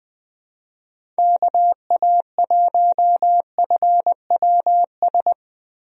Morse code, heard as a keyed tone, about 20 wpm, 700 Hz. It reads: KA1FWS
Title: USA Call Signs - 6 Characters - Repeated 2x - 20wpm